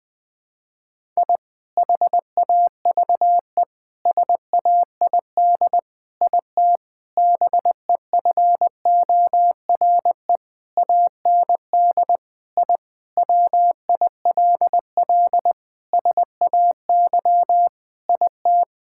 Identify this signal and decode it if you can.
Morse code, heard as a keyed tone, about 20 wpm, 700 Hz. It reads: I HAVE SAID IT BEFORE AND I WILL SAY IT
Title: Sentences with 8-Word Phrases - 20wpm